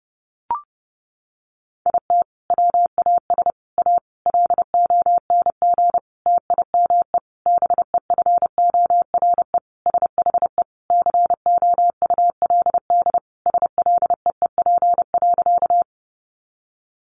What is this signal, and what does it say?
Morse code, heard as a keyed tone, about 30 wpm, 700 Hz. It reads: IT WAS A LONG TIME BEFORE SHE COULD SLEEP.
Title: Sentences from Top 500 Words 30wpm